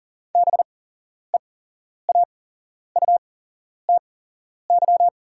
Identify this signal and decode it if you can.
Morse code, heard as a keyed tone, about 40 wpm, spaced as for 11 wpm, 690 Hz. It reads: BEAUTY
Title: 8x Character Spacing - Top 500 Words - 40wpm